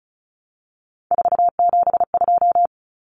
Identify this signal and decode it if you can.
Morse code, heard as a keyed tone, about 34 wpm, 700 Hz. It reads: 472
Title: Numbers - 3-Digit 35wpm